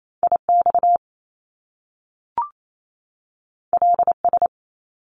Morse code, heard as a keyed tone, about 28 wpm, 700 Hz. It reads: IX LS
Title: Top Two Letter Combinations in Words 28wpm